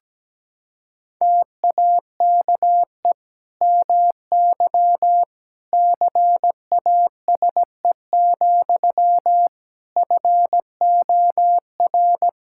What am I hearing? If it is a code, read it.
Morse code, heard as a keyed tone, about 17 wpm, 700 Hz. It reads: TAKE MY CASE, FOR
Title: Sentences from Top 1000 Words - Review Part 5 - 17wpm